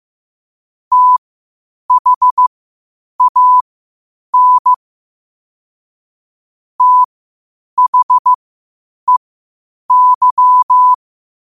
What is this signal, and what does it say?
Morse code, heard as a keyed tone, about 15 wpm, spaced as for 8 wpm, 1000 Hz. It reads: THAN THEY
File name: qotdfiles / 8wpm